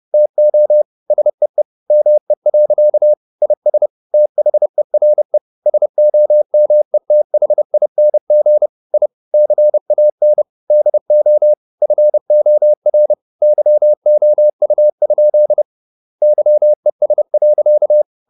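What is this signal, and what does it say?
Morse code, heard as a keyed tone, about 30 wpm, 600 Hz. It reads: TO SEE ME. IS THERE SOMETHING I CAN DO FOR YOU? YES.
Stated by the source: Morse code at 30 wpm, the novel Never by Ken Follett in English, from CW med Gnister.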